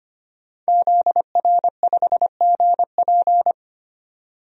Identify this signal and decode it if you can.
Morse code, heard as a keyed tone, about 25 wpm, 700 Hz. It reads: ZR5GP
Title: DXCC-100 Call Signs - 5 Characters - 25wpm